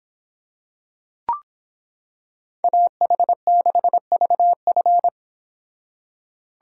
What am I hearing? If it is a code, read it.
Morse code, heard as a keyed tone, about 26 wpm, 700 Hz. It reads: AH6VF